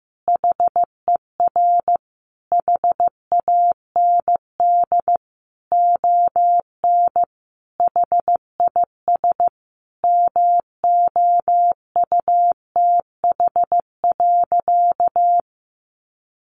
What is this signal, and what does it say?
Morse code, heard as a keyed tone, about 15 wpm, 700 Hz. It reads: HER HAND ON HIS MOUTH.
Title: Sentences from Top 1500 Words 15wpm